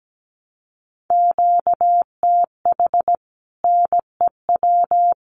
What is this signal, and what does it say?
Morse code, heard as a keyed tone, about 17 wpm, 700 Hz. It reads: QTH NEW
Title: Common QSO Phrases Encore 17wpm